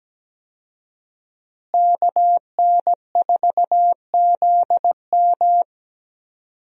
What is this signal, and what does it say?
Morse code, heard as a keyed tone, about 17 wpm, 700 Hz. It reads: KN4ZM
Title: USA Call Signs - 5 Characters - Repeated 2x - 17wpm